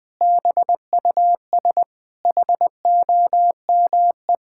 Morse code, heard as a keyed tone, about 20 wpm, 700 Hz. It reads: BUS HOME
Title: Sentences from Top 1000 Words - Review Part 6 - 20wpm